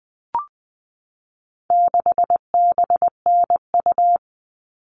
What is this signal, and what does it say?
Morse code, heard as a keyed tone, about 20 wpm, 700 Hz. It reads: 6BNU
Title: All Cut Numbers - 4-Digit 20wpm